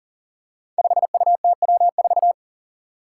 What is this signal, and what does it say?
Morse code, heard as a keyed tone, about 40 wpm, 700 Hz. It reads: 5UTW4
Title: Five Challenging Letter-Number Combinations 40wpm